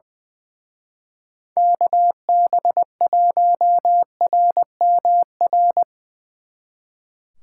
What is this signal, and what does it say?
Morse code, heard as a keyed tone, about 20 wpm, 700 Hz. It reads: KB1RMR
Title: Speed Racing - USA 2x3 Call Signs 20wpm